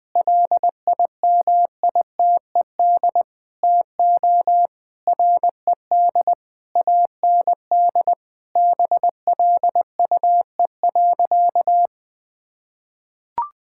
Morse code, heard as a keyed tone, about 21 wpm, 700 Hz. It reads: LIMITED TO RED AND BLUE.
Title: Sentences with 6-Word Phrases - 20wpm